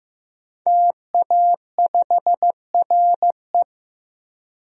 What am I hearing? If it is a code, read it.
Morse code, heard as a keyed tone, about 15 wpm, 700 Hz. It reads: TA5RE
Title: DXCC-100 Call Signs - 5 Characters - 15wpm